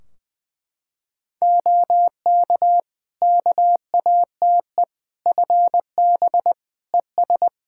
Morse code, heard as a keyed tone, about 20 wpm, 700 Hz. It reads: OK KATE FB ES
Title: Practice QSO - Exchange 2 - Info - 20wpm